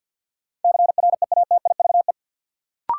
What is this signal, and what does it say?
Morse code, heard as a keyed tone, about 50 wpm, 700 Hz. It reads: CREATIVE
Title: Sentences from Top 1000 Words - Review Part 5 - New Word Drill - 50wpm